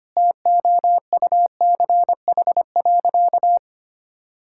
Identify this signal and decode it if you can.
Morse code, heard as a keyed tone, about 25 wpm, 700 Hz. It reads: TOUCH.
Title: Sentences from Top 1600 Words 25wpm